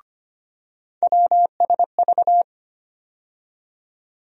Morse code, heard as a keyed tone, about 25 wpm, 700 Hz. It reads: WSV
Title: USA Call Sign Postfixes - 3 Letters 25wpm